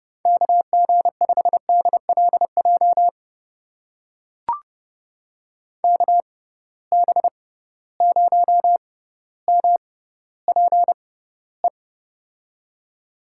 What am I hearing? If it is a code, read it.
Morse code, heard as a keyed tone, about 30 wpm, spaced as for 15 wpm, 700 Hz. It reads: KG5DLJ KB0MPE
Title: USA Call Signs - 6 Characters - 6x Character Spacing - 30wpm